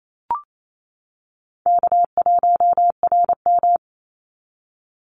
Morse code, heard as a keyed tone, about 28 wpm, 700 Hz. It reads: K1RM
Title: Easy Call Signs - 28wpm